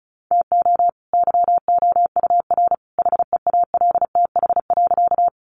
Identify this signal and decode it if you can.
Morse code, heard as a keyed tone, about 34 wpm, 700 Hz. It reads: TO YOUR HEALTH.
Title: Sentences from Top 600 Words 35wpm